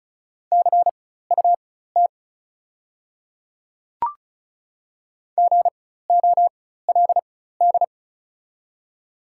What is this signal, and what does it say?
Morse code, heard as a keyed tone, about 36 wpm, spaced as for 16 wpm, 690 Hz. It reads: CUT GOLD
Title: 4x Character Spacing - Top 500 Words - 35wpm